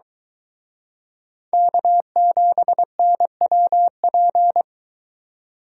Morse code, heard as a keyed tone, about 23 wpm, 700 Hz. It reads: K7NWP